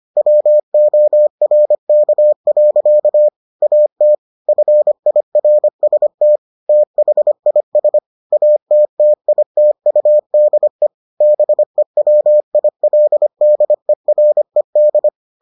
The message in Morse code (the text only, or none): WORK. AT FIRST THIS ATTITUDE BEWILDERED